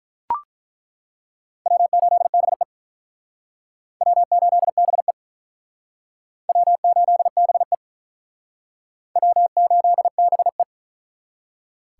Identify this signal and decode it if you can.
Morse code, heard as a keyed tone, about 48 wpm, 700 Hz. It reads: W8BE W8BE W8BE W8BE